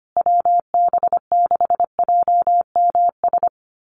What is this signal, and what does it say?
Morse code, heard as a keyed tone, about 25 wpm, 700 Hz. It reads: WB6JMS
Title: USA Call Signs Repeated 2x 25wpm